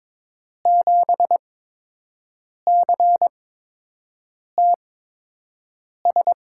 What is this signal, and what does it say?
Morse code, heard as a keyed tone, about 22 wpm, spaced as for 6 wpm, 700 Hz. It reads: 7CTS
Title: USA Call Signs - 5 Characters - 8x Character Spacing - 22wpm